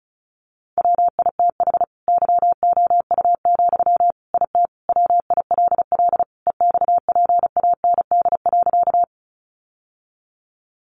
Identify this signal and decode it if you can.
Morse code, heard as a keyed tone, about 34 wpm, 700 Hz. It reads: WITH YOU, IT WILL EXPAND.
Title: Sentences from Top 1400 Words 35wpm